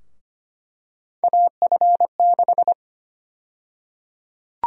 Morse code, heard as a keyed tone, about 25 wpm, 700 Hz. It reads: AF6
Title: USA Call Sign Prefixes - Repeated 2x - 25wpm